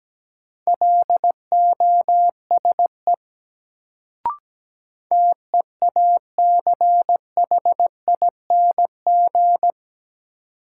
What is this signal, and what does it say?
Morse code, heard as a keyed tone, about 17 wpm, 700 Hz. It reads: LOSE TEACHING
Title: Sentences from Top 1100 Words - New Word Drill - 17wpm